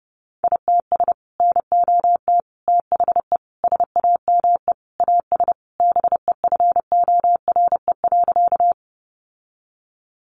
Morse code, heard as a keyed tone, about 30 wpm, 700 Hz. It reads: ITS NOT THE SAME AS BEFORE.
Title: Sentences from Top 600 Words 30wpm